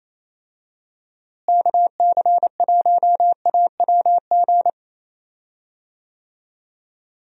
Morse code, heard as a keyed tone, about 27 wpm, 700 Hz. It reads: KC1AWG
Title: USA Call Signs - 6 Characters - Repeated 3x - 28wpm